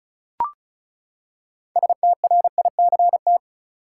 Morse code, heard as a keyed tone, about 36 wpm, 700 Hz. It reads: STRICT
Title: Six Letter Words - 35wpm